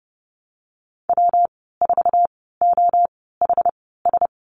Morse code, heard as a keyed tone, about 30 wpm, spaced as for 17 wpm, 700 Hz. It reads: W4OHS
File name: Lesson 19g - Letter w - Call Signs 30wpm